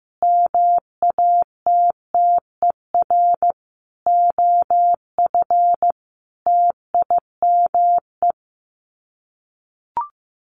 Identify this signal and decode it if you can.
Morse code, heard as a keyed tone, about 15 wpm, 700 Hz. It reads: MATTER OF TIME
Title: Sets of 5 Words Encore 15wpm